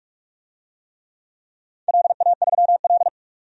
Morse code, heard as a keyed tone, about 42 wpm, 700 Hz. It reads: LA3L
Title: DXCC-100 Call Signs - 4 Characters - 45wpm